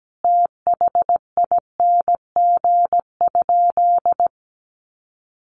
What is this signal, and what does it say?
Morse code, heard as a keyed tone, about 17 wpm, 700 Hz. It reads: THING?
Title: Sentences from Top 200 Words 17wpm